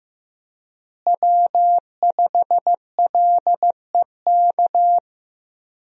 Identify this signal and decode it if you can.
Morse code, heard as a keyed tone, about 15 wpm, 700 Hz. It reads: W5LEK
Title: USA Call Signs - 5 Characters - Repeated 3x - 15wpm